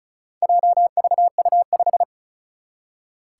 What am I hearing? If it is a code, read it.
Morse code, heard as a keyed tone, about 36 wpm, 700 Hz. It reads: JVU5
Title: Four Challenging Letter-Number Combinations 35wpm